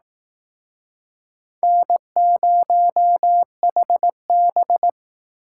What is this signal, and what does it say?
Morse code, heard as a keyed tone, about 19 wpm, 700 Hz. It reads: N0HB
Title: Speed Racing - USA 1x2 Call Signs 15wpm